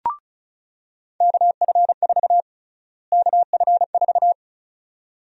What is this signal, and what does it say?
Morse code, heard as a keyed tone, about 36 wpm, 700 Hz. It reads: KF4 KF4
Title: USA Call Sign Prefixes - Repeated 2x - 35wpm